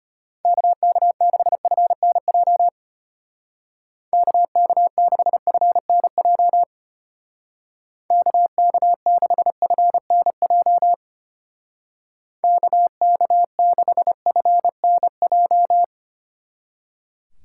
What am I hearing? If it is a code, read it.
Morse code, heard as a keyed tone, about 32 wpm, 700 Hz. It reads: KK6FNJ KK6FNJ KK6FNJ KK6FNJ